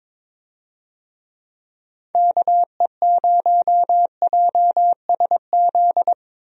Morse code, heard as a keyed tone, about 22 wpm, 700 Hz. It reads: KE0JSZ
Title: USA Call Signs - 6 Characters - Repeated 2x - 22wpm